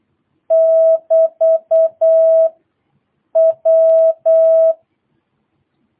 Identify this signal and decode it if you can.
Morse code, heard as a keyed tone, about 8 wpm, 650 Hz. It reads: <BT> W
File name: jh8cbh / 2022cw